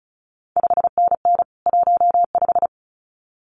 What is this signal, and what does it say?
Morse code, heard as a keyed tone, about 36 wpm, 700 Hz. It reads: 5NN 15
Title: CQ WW DX Contest Exchange Encore - 35wpm